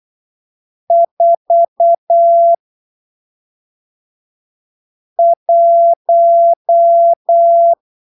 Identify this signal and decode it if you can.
Morse code, heard as a keyed tone, about 8 wpm, spaced as for 3 wpm, 680 Hz. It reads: 41
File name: arquivos / texto / texto 12 08ppm